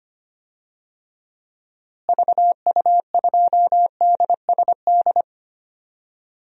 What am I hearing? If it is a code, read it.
Morse code, heard as a keyed tone, about 25 wpm, 700 Hz. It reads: VU2DSD
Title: DXCC-100 Call Signs - 6 Characters - Repeated 2x - 25wpm